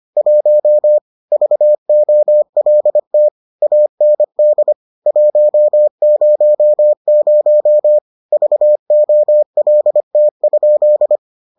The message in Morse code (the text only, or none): 1 VOLT AND 100 VOLT?